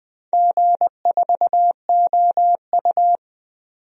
Morse code, heard as a keyed tone, about 21 wpm, 700 Hz. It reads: G4OU
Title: DXCC-100 Call Signs - 4 Characters - Repeated 3x - 20wpm